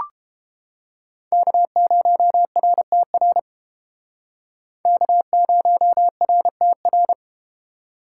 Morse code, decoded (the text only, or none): K0RTR K0RTR